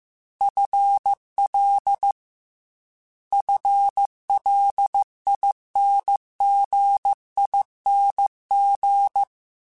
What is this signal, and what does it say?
Morse code, heard as a keyed tone, about 15 wpm, 800 Hz. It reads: FL FLINGING
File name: FL / FL mp3s / FL Prefixes 3x2 @13